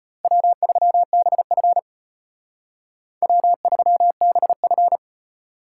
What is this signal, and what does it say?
Morse code, heard as a keyed tone, about 32 wpm, 700 Hz. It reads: W3BF W3BF